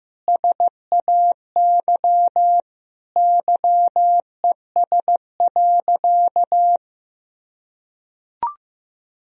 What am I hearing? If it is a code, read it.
Morse code, heard as a keyed tone, about 15 wpm, 700 Hz. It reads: SAY YES.
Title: Sentences from Top 500 Words and Abbreviations 15wpm